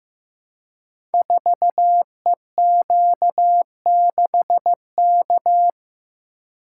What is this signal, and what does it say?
Morse code, heard as a keyed tone, about 15 wpm, 700 Hz. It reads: 4EQ6K
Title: Code Groups - 5 Letter:Numbers - 15wpm